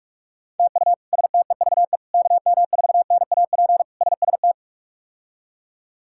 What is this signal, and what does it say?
Morse code, heard as a keyed tone, about 46 wpm, 700 Hz. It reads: TU STEVE KM4NAP SST
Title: K1USN SST QSO Practice - 45wpm